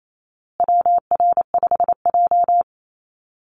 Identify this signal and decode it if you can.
Morse code, heard as a keyed tone, about 28 wpm, 700 Hz. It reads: WR5J
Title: Call Signs Repeated 2x - 28wpm